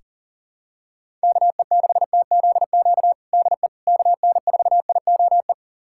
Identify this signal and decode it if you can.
Morse code, heard as a keyed tone, about 40 wpm, 700 Hz. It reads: KE6TZQ DE KN4IOE
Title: USA Callsign DE Callsign 40wpm